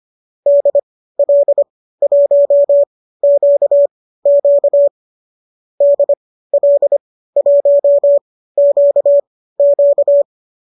Morse code, heard as a keyed tone, about 25 wpm, spaced as for 15 wpm, 590 Hz. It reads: DL1QQ DL1QQ
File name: qso219 15